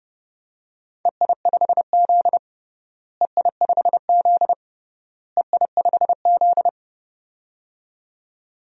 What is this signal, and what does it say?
Morse code, heard as a keyed tone, about 30 wpm, 700 Hz. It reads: EI5Z EI5Z EI5Z
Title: DXCC-100 Call Signs - 4 Characters - Repeated 3x - 30wpm